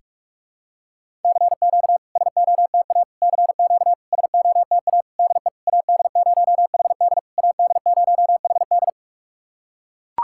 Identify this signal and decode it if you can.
Morse code, heard as a keyed tone, about 44 wpm, 690 Hz. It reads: CQ SOTA CQ SOTA DE AD0HD AD0HD
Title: SOTA QSO Practice 45wpm